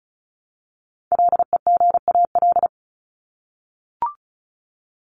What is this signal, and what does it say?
Morse code, heard as a keyed tone, about 34 wpm, 700 Hz. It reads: LEGAL
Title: Top 1001-1250 Words 35wpm